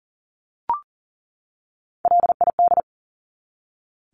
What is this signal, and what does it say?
Morse code, heard as a keyed tone, about 40 wpm, 700 Hz. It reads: LID
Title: Abbreviations 40wpm